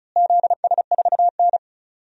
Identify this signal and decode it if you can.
Morse code, heard as a keyed tone, about 34 wpm, 700 Hz. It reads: ZS4N
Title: DXCC-100 Call Signs - 4 Characters - Repeated 3x - 35wpm